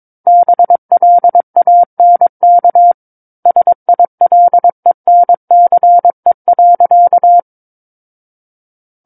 Morse code, heard as a keyed tone, about 22 wpm, 700 Hz. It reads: BLANK SILENCE.
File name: Book One / 13-Chapter 13 - 22 WPM